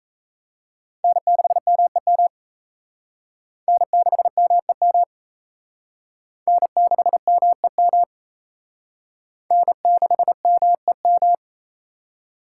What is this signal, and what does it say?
Morse code, heard as a keyed tone, about 36 wpm, 700 Hz. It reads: N6MEM N6MEM N6MEM N6MEM